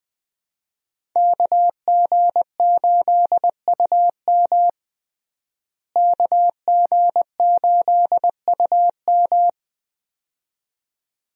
Morse code, heard as a keyed tone, about 20 wpm, 700 Hz. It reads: KG8UM KG8UM
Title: USA Call Signs - 5 Characters - Repeated 2x - 20wpm